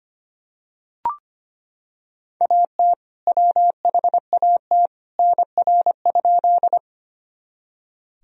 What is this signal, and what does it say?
Morse code, heard as a keyed tone, about 25 wpm, 700 Hz. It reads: AT WHAT NR?
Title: Sentences from Top 100 Words and Abbreviations 25wpm